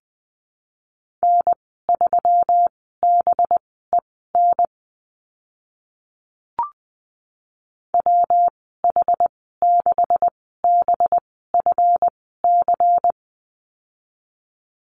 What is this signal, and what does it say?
Morse code, heard as a keyed tone, about 20 wpm, spaced as for 14 wpm, 700 Hz. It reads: N3BEN WH6BFC
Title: 193 Lesson 31g - Letter v - Call Signs 20wpm